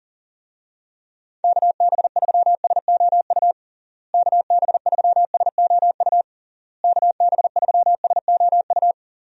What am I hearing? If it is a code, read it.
Morse code, heard as a keyed tone, about 40 wpm, 700 Hz. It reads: KB3SOU KB3SOU KB3SOU
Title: USA Call Signs - 6 Characters - Repeated 3x - 40wpm